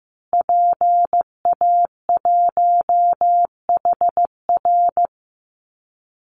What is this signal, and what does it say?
Morse code, heard as a keyed tone, about 15 wpm, 700 Hz. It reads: PA1HR
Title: Call Signs Repeated 2x - 15wpm